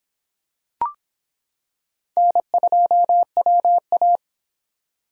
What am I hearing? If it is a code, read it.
Morse code, heard as a keyed tone, about 26 wpm, 700 Hz. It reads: N2WA